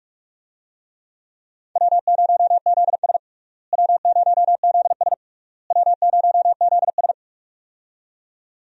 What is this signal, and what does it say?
Morse code, heard as a keyed tone, about 41 wpm, 700 Hz. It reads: W0ZS W0ZS W0ZS
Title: USA Call Signs - 4 Characters - Repeated 3x - 45wpm